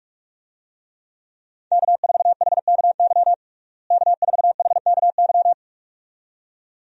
Morse code, heard as a keyed tone, about 42 wpm, 700 Hz. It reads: K4HKY K4HKY
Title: USA Call Signs - 5 Characters - Repeated 2x - 45wpm